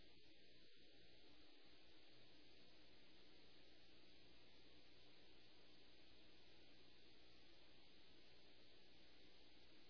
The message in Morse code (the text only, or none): none